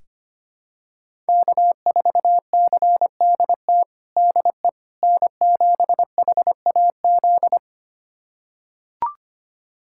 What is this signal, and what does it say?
Morse code, heard as a keyed tone, about 25 wpm, 700 Hz. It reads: K4CDT DE N7HAZ
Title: USA Callsign DE Callsign - 5 Characters - 25wpm